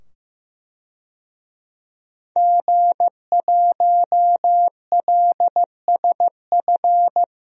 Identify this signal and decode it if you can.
Morse code, heard as a keyed tone, about 15 wpm, 700 Hz. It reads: G1LSF
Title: DXCC-100 Call Signs - 5 Characters - 15wpm